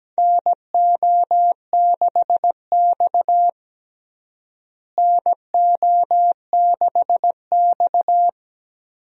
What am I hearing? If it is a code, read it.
Morse code, heard as a keyed tone, about 17 wpm, 700 Hz. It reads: NO6X NO6X
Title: USA Call Signs - 4 Characters - Repeated 3x - 17wpm